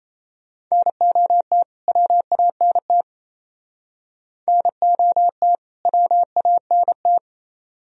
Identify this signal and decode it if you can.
Morse code, heard as a keyed tone, about 35 wpm, 700 Hz. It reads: NOT WANT NOT WANT
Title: Speed-Racing - Sets of 2 Words 28wpm